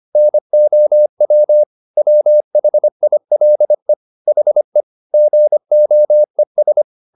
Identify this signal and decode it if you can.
Morse code, heard as a keyed tone, about 25 wpm, 600 Hz. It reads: NOW WHILE HE GOES